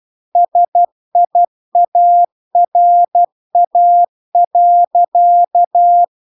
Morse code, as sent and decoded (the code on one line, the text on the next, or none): ... .. .- .-. .- .-.-.-
SIARA.